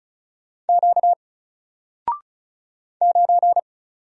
Morse code, heard as a keyed tone, about 35 wpm, 700 Hz. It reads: Q 9
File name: Single-Letter-Number-35wpm